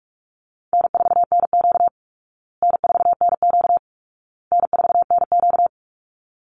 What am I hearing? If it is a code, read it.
Morse code, heard as a keyed tone, about 44 wpm, 700 Hz. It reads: N4NQ N4NQ N4NQ
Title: Easy Call Signs Repeated 3x - 45wpm